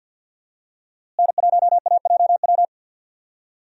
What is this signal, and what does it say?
Morse code, heard as a keyed tone, about 53 wpm, 700 Hz. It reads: N1AJW
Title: USA Call Signs - 5 Characters - Repeated 2x - 50wpm